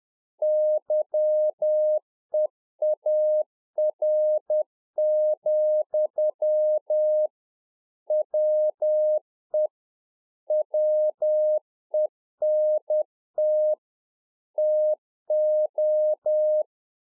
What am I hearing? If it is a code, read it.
Morse code, heard as a keyed tone, about 10 wpm, 610 Hz. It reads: YEAR, WE WENT TO